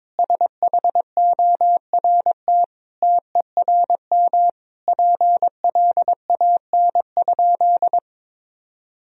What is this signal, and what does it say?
Morse code, heard as a keyed tone, about 21 wpm, 700 Hz. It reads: SHORT TERM PLAN?
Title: Sentences from Top 1000 Words - Review Part 4 - 22wpm